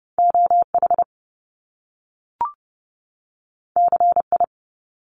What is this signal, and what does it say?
Morse code, heard as a keyed tone, about 30 wpm, 700 Hz. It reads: OH CI
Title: Top Two Letter Combinations in Words 30wpm